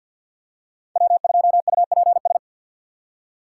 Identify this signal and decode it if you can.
Morse code, heard as a keyed tone, about 46 wpm, 700 Hz. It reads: W2UPS
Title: USA Call Signs - 5 Characters - 6x Character Spacing - 50wpm